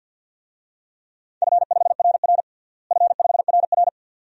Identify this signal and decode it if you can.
Morse code, heard as a keyed tone, about 52 wpm, 700 Hz. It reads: F5RR F5RR
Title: DXCC-100 Call Signs - 4 Characters - Repeated 2x - 50wpm